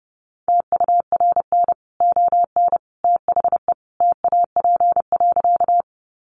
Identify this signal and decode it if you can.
Morse code, heard as a keyed tone, about 29 wpm, 700 Hz. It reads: TURN ON THE TAP.
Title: Sentences from Top 1400 Words 30wpm